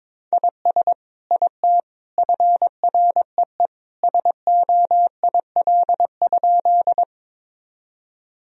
Morse code, heard as a keyed tone, about 22 wpm, 700 Hz. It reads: IS IT FREE SOIL?
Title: Sentences from Top 1000 Words - Review Part 4 - 22wpm